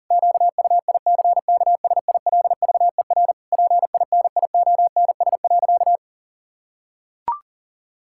Morse code, heard as a keyed tone, about 40 wpm, 700 Hz. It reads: QUICKSILVER PINIONS.
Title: Pangram Challenge v2 - 40wpm